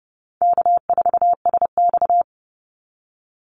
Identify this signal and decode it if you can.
Morse code, heard as a keyed tone, about 30 wpm, 700 Hz. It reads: K4SX
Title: Call Signs - 30wpm